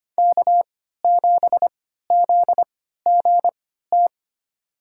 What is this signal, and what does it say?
Morse code, heard as a keyed tone, about 25 wpm, spaced as for 14 wpm, 700 Hz. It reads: K7ZGT
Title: USA Call Signs - 5 Characters - 3x Character Spacing - 25wpm